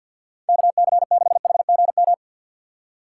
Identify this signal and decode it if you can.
Morse code, heard as a keyed tone, about 60 wpm, 700 Hz. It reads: KC6HGM
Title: Speed Racing - USA 2x3 Call Signs 50wpm